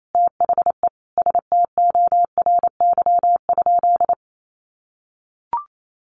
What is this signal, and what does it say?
Morse code, heard as a keyed tone, about 28 wpm, 700 Hz. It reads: THE STORY?
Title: Sentences from Top 1100 Words 28wpm